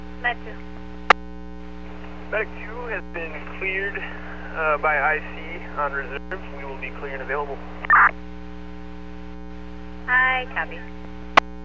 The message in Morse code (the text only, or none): none